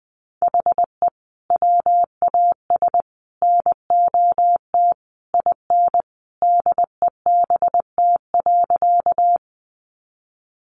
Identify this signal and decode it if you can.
Morse code, heard as a keyed tone, about 20 wpm, 700 Hz. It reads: HE WAS NOT IN DEBT.
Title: Sentences from Top 1300 Words 20wpm